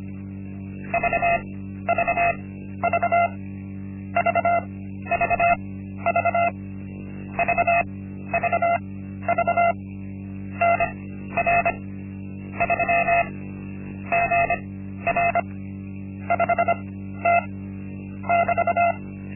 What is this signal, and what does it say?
Morse code, heard as a keyed tone, about 25 wpm, spaced as for 13 wpm, 690 Hz. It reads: VVV VVV VVV NR 3 GR 5T <BT>
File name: Linksmakalnis-1